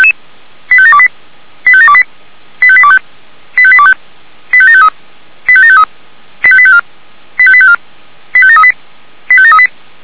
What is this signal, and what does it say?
Five-tone selective call (ZVEI), heard as quick runs of five tones, tones of about 70 ms. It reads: 85728 85728 85825 85825 85753 85753 85854 85854 85738 85738
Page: /464/SDR/ZVEI/85758/